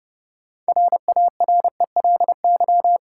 Morse code, heard as a keyed tone, about 30 wpm, 700 Hz. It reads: RARELY
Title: Sentences from Top 1400 Words - New Word Drill Alt - 30wpm